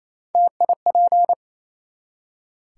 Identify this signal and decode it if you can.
Morse code, heard as a keyed tone, about 28 wpm, 700 Hz. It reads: TIP
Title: Three Letter Words - 28wpm